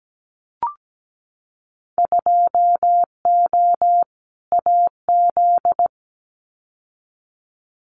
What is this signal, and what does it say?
Morse code, heard as a keyed tone, about 17 wpm, 700 Hz. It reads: 2O AZ
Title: Winter Field Day Exchanges 17wpm